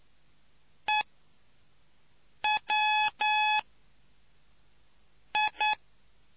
Morse code, heard as a keyed tone, about 9 wpm, spaced as for 4 wpm, 850 Hz. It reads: EWI